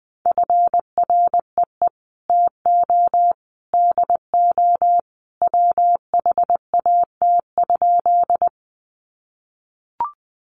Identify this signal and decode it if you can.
Morse code, heard as a keyed tone, about 20 wpm, 700 Hz. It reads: FREE TO DO WHAT?
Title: Sentences from Top 500 Words 20wpm